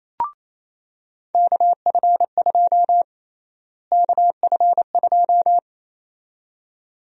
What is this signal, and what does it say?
Morse code, heard as a keyed tone, about 27 wpm, 700 Hz. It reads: KF2 KF2
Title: USA Call Sign Prefixes - Repeated 2x - 28wpm